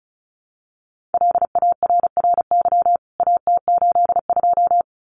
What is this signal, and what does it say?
Morse code, heard as a keyed tone, about 34 wpm, 700 Hz. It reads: LARRY AT82
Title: CWT Exchange 35wpm